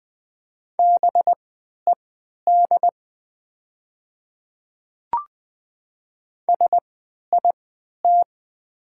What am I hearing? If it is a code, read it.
Morse code, heard as a keyed tone, about 21 wpm, spaced as for 11 wpm, 690 Hz. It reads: BED SIT
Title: 3x Character Spacing - Top 500 Words - 20wpm